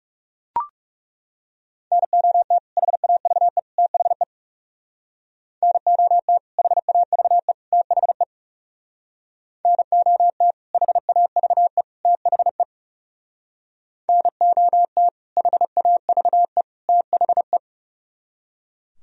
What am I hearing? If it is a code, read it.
Morse code, heard as a keyed tone, about 36 wpm, 700 Hz. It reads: NOT HAVE THE NOT HAVE THE NOT HAVE THE NOT HAVE THE
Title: Speed-Racing - Sets of 3 Words 30wpm